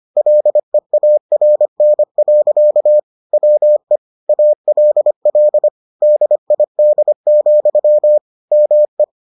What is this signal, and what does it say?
Morse code, heard as a keyed tone, about 25 wpm, 600 Hz. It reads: LEARN. WE ALL DID, ME